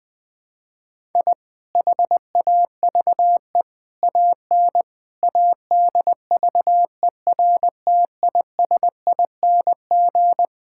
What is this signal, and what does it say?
Morse code, heard as a keyed tone, about 20 wpm, 700 Hz. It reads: I HAVE AN ADVERTISING
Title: Sentences from Top 1000 Words - Review Part 6 - 20wpm